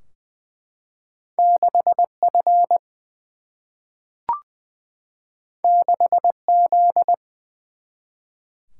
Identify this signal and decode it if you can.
Morse code, heard as a keyed tone, about 20 wpm, 700 Hz. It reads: 6 F 6 Z
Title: Single-Letter Single-Number 20wpm